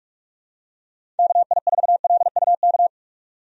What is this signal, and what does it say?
Morse code, heard as a keyed tone, about 42 wpm, 700 Hz. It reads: KI4LUK
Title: USA Call Signs - 6 Characters - 45wpm